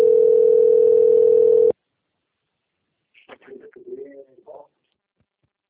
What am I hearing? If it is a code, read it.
Ringback tone (North America), 440 + 480 Hz.
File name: rg-1600-unknown- - - .16146